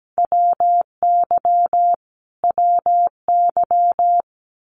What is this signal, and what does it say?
Morse code, heard as a keyed tone, about 17 wpm, 700 Hz. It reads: WY WY
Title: US States Repeated 3x 17wpm